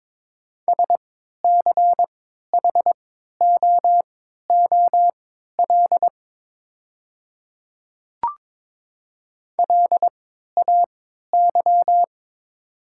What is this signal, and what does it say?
Morse code, heard as a keyed tone, about 22 wpm, spaced as for 13 wpm, 700 Hz. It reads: SCHOOL LAY
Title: 3x Character Spacing - Top 500 Words - 22wpm